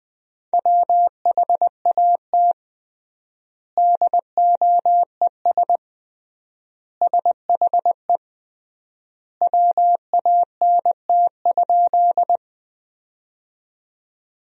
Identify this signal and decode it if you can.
Morse code, heard as a keyed tone, about 20 wpm, 700 Hz. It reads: WHAT DOES SHE WANT?
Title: 3x Word-Spacing - Sentences from Top 200 Words - 20wpm